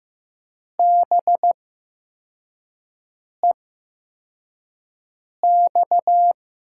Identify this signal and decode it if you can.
Morse code, heard as a keyed tone, about 15 wpm, spaced as for 4 wpm, 700 Hz. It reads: BEX
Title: USA Call Signs - 6 Characters - 8x Character Spacing - 15wpm